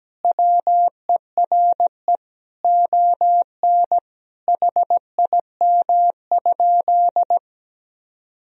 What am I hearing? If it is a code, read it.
Morse code, heard as a keyed tone, about 17 wpm, 700 Hz. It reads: WERE ON HIM?
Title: Sentences from Top 1000 Words - Review Part 2 - 17wpm